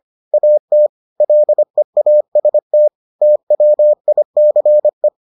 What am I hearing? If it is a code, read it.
Morse code, heard as a keyed tone, about 25 wpm, 600 Hz. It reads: AT LEAST TWICE